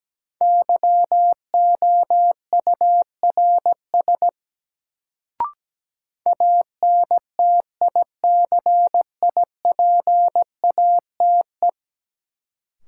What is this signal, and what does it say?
Morse code, heard as a keyed tone, about 17 wpm, 700 Hz. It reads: YOURS ANTICIPATE
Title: Sentences from Top 1500 Words - New Word Drill Alt - 17wpm